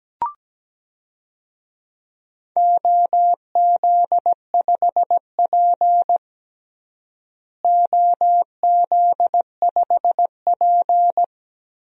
Morse code, heard as a keyed tone, about 17 wpm, 700 Hz. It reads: OZ5P OZ5P
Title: DXCC-100 Call Signs - 4 Characters - Repeated 3x - 17wpm